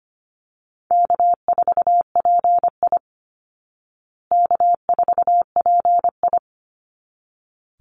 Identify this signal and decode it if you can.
Morse code, heard as a keyed tone, about 25 wpm, 700 Hz. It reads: K4PI K4PI
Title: Call Signs Repeated 3x - 25wpm